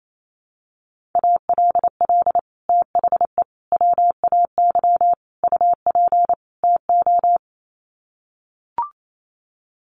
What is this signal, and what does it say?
Morse code, heard as a keyed tone, about 28 wpm, 700 Hz. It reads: ALL THE WAY UP TO
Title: Sets of 5 Words Encore 28wpm